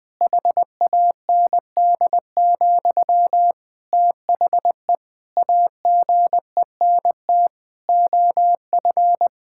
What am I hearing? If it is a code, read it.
Morse code, heard as a keyed tone, about 20 wpm, 700 Hz. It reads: HAND, THE AGENT OF
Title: Sentences from Top 1600 Words 20wpm